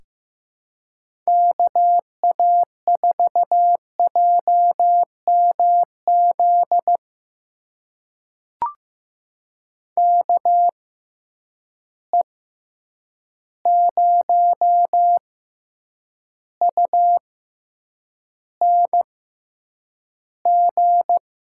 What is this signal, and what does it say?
Morse code, heard as a keyed tone, about 15 wpm, spaced as for 8 wpm, 700 Hz. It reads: KA4JMZ KE0UNG
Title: USA Call Signs - 6 Characters - 6x Character Spacing - 15wpm